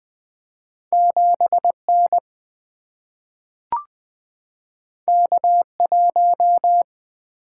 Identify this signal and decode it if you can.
Morse code, heard as a keyed tone, about 20 wpm, 700 Hz. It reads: 7 N K 1
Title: Single-Letter Single-Number 20wpm